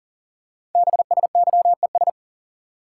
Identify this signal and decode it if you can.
Morse code, heard as a keyed tone, about 42 wpm, 690 Hz. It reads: BSYES
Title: Code Groups - 5 Letters - 40wpm